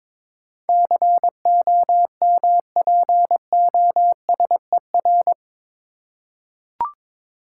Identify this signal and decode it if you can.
Morse code, heard as a keyed tone, about 22 wpm, 700 Hz. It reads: COMPOSER
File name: practice / 22wpm / Eight Letter Words - 22wpm